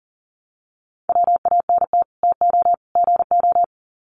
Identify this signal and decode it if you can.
Morse code, heard as a keyed tone, about 40 wpm, 700 Hz. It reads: WANT TO GO
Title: Sets of 3 Words 40wpm